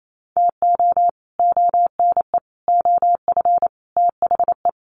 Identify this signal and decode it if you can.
Morse code, heard as a keyed tone, about 28 wpm, 700 Hz. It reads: TO ONE OF THE
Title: Sets of 4 Words 28wpm